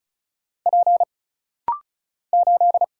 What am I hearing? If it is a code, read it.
Morse code, heard as a keyed tone, about 34 wpm, 700 Hz. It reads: P 8
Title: Single Letter-Number Alt - Rapid-Fire 35wpm